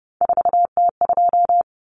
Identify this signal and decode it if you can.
Morse code, heard as a keyed tone, about 29 wpm, 700 Hz. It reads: 4T2
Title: Common Cut Numbers - 3-Digit 30wpm